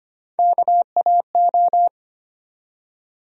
Morse code, heard as a keyed tone, about 25 wpm, 700 Hz. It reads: KAO
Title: USA Call Sign Postfixes - 3 Letters 25wpm